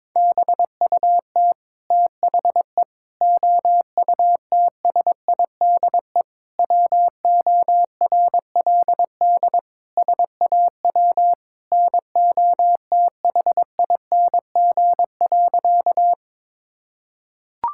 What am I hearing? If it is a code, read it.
Morse code, heard as a keyed tone, about 22 wpm, 700 Hz. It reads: BUT THE OUTSIDE WORLD SAW NOTHING.
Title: Sentences from Top 1000 Words - Review Part 6 - 22wpm